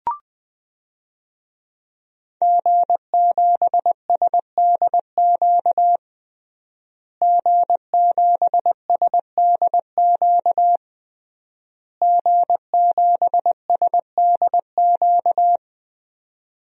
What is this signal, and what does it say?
Morse code, heard as a keyed tone, about 20 wpm, 700 Hz. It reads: G7SDQ G7SDQ G7SDQ
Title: DXCC-100 Call Signs - 5 Characters - Repeated 3x - 20wpm